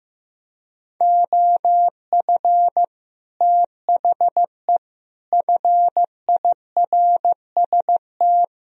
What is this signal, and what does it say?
Morse code, heard as a keyed tone, about 15 wpm, 700 Hz. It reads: OF THE FIRST
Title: Speed-Racing - Sets of 3 Words 15wpm